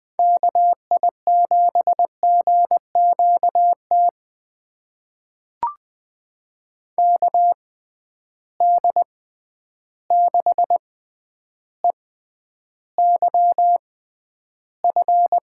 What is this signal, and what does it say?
Morse code, heard as a keyed tone, about 20 wpm, spaced as for 10 wpm, 700 Hz. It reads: KI7GQT KD6EYF
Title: USA Call Signs - 6 Characters - 6x Character Spacing - 20wpm